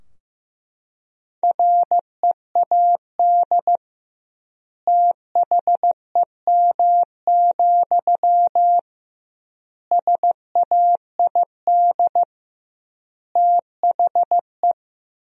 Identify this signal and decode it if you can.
Morse code, heard as a keyed tone, about 15 wpm, 700 Hz. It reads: READ THEM, SAID THE
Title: 2x Word-Spacing - Sentences from Top 400 Words - 15wpm